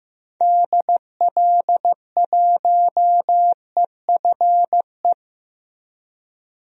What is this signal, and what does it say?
Morse code, heard as a keyed tone, about 15 wpm, 700 Hz. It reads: DL1EFE
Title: DXCC-100 Call Signs - 6 Characters - Repeated 2x - 15wpm